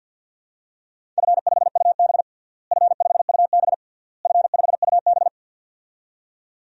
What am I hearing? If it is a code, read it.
Morse code, heard as a keyed tone, about 51 wpm, 700 Hz. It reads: F5UB F5UB F5UB
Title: DXCC-100 Call Signs - 4 Characters - Repeated 3x - 50wpm